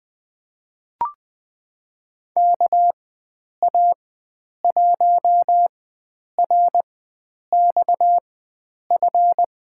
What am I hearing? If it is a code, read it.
Morse code, heard as a keyed tone, about 21 wpm, spaced as for 9 wpm, 700 Hz. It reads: KA1RXF
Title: USA Call Signs - 4x Character Spacing - 20wpm